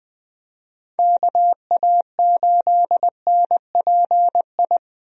Morse code, heard as a keyed tone, about 21 wpm, 700 Hz. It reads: KA8NPI
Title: USA Call Signs - 6 Characters - Repeated 2x - 20wpm